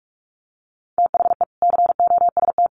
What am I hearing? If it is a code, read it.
Morse code, heard as a keyed tone, about 44 wpm, 700 Hz. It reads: THE COST
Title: Sets of 3 Words 45wpm